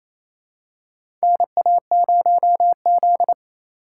Morse code, heard as a keyed tone, about 28 wpm, 700 Hz. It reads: NA0Z
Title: USA Call Signs - 4 Characters - 4x Character Spacing - 28wpm